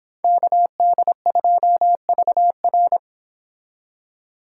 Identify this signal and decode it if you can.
Morse code, heard as a keyed tone, about 26 wpm, 700 Hz. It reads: KD2VR